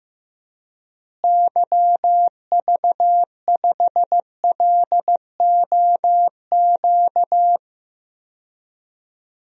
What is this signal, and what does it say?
Morse code, heard as a keyed tone, about 15 wpm, 700 Hz. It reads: YV5LOQ
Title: DXCC-100 Call Signs - 6 Characters - Repeated 2x - 15wpm